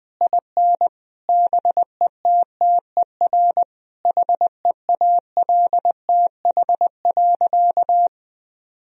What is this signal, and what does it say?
Morse code, heard as a keyed tone, about 21 wpm, 700 Hz. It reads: IN BETTER HEALTH.
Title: Sentences from Top 1000 Words - Review Part 3 - 20wpm